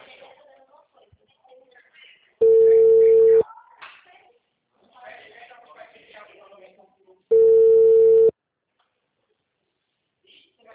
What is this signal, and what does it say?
Ringback tone (China, 450 Hz), with speech.
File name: out- -2009- - - .70071